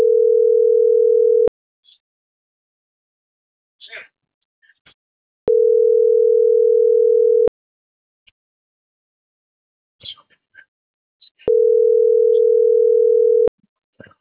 Ringback tone (North America), 440 + 480 Hz.